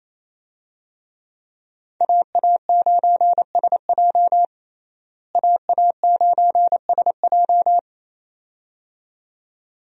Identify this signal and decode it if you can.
Morse code, heard as a keyed tone, about 28 wpm, 700 Hz. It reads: AA9SJ AA9SJ
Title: USA Call Signs - 5 Characters - Repeated 2x - 28wpm